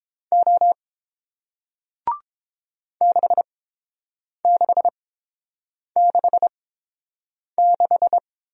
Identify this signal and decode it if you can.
Morse code, heard as a keyed tone, about 30 wpm, spaced as for 7 wpm, 690 Hz. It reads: O 6666